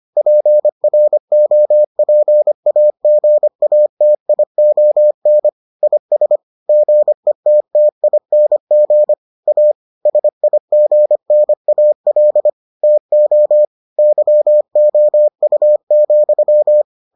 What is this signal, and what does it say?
Morse code, heard as a keyed tone, about 25 wpm, 600 Hz. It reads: PROPAGATION IS GETTING A SIGNAL TO YOU,